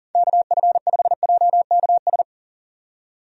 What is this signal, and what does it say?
Morse code, heard as a keyed tone, about 40 wpm, 700 Hz. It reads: KF5JKS
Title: USA Call Signs - 6 Characters - 40wpm